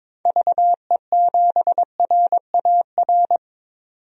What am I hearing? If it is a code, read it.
Morse code, heard as a keyed tone, about 22 wpm, 700 Hz. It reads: VE7RAR
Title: DXCC-100 Call Signs - 6 Characters - 22wpm